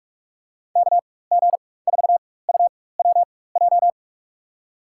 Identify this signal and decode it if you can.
Morse code, heard as a keyed tone, about 43 wpm, spaced as for 21 wpm, 700 Hz. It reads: KG4UWJ
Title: USA Call Signs - 6 Characters - 4x Character Spacing - 45wpm